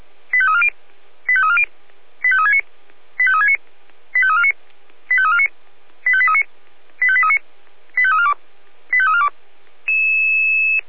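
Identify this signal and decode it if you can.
Five-tone selective call (ZVEI), heard as quick runs of five tones, tones of about 70 ms. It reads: 85340 85340 85369 85369 85349 85349 85839 85839 85342 85342
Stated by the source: ZVEI five-tone selective call received with an SDR.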